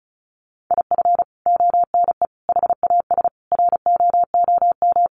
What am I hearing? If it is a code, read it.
Morse code, heard as a keyed tone, about 34 wpm, 700 Hz. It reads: IF ONE HAS ROOM
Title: Sentences from Top 1400 Words 35wpm